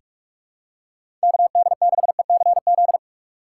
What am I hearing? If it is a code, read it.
Morse code, heard as a keyed tone, about 46 wpm, 700 Hz. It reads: KD6ECZ
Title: USA Call Signs - 3x Character Spacing - 45wpm